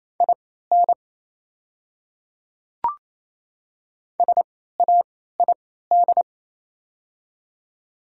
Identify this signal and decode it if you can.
Morse code, heard as a keyed tone, about 28 wpm, spaced as for 16 wpm, 700 Hz. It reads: IN SAID
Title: 3x Character Spacing - Top 100 Words - 28wpm